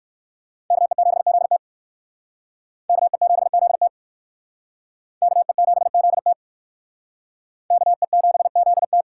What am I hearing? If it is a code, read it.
Morse code, heard as a keyed tone, about 56 wpm, 700 Hz. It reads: KE7ZT KE7ZT KE7ZT KE7ZT
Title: Speed Racing - USA 2x2 Call Signs 45wpm